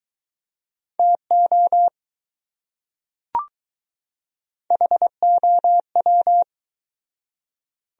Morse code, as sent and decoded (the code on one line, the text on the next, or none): - --- / .... --- .--
TO HOW